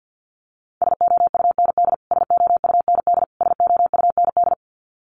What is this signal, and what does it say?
Morse code, heard as a keyed tone, about 50 wpm, 700 Hz. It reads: SOUND SOUND SOUND
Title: Top 300 Words Repeated 50wpm